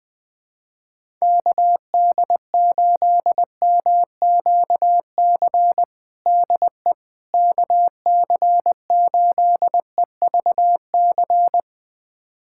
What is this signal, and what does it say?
Morse code, heard as a keyed tone, about 20 wpm, 700 Hz. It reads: KD8MQC DE KC8EVC
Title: USA Callsign DE Callsign 20wpm